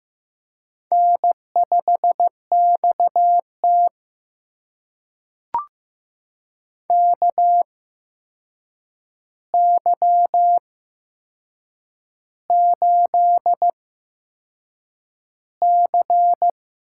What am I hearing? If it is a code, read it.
Morse code, heard as a keyed tone, about 15 wpm, spaced as for 6 wpm, 700 Hz. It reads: N5XT KY8C
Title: USA Call Signs - 4 Characters - 8x Character Spacing - 15wpm